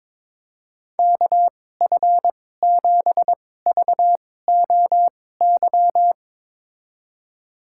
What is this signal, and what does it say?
Morse code, heard as a keyed tone, about 21 wpm, spaced as for 16 wpm, 700 Hz. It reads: KF7VOY
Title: USA Call Signs - 2x Character Spacing - 22wpm